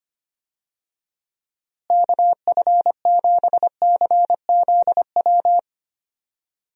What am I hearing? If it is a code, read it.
Morse code, heard as a keyed tone, about 25 wpm, 700 Hz. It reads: KF7CZW
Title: USA Call Signs - 6 Characters - Repeated 3x - 25wpm